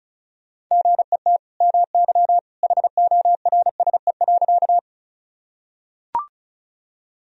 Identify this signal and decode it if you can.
Morse code, heard as a keyed tone, about 36 wpm, 700 Hz. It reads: GET MY HORSE.
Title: Sentences from Top 1000 Words - Review Part 1 - 35wpm